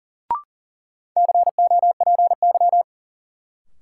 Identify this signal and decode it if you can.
Morse code, heard as a keyed tone, about 40 wpm, 700 Hz. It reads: COPY
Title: Sentences from Top 1400 Words - New Word Drill Alt - 40wpm